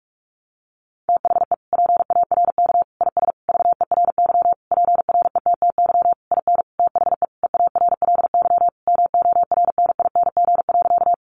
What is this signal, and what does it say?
Morse code, heard as a keyed tone, about 44 wpm, 700 Hz. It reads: THE PARK IS VERY PRETTY IN THE EARLY MORNING.
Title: Sentences from Top 1300 Words 45wpm